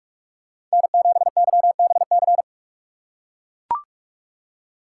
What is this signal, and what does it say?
Morse code, heard as a keyed tone, about 43 wpm, 700 Hz. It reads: N7YBC
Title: USA Call Signs - 5 Characters - 3x Character Spacing - 45wpm